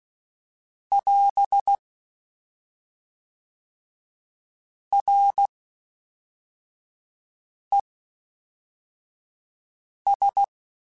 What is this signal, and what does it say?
Morse code, heard as a keyed tone, about 16 wpm, 800 Hz. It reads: <AS> R E S
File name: Combinations / Combinations-mp3s / Combinations@10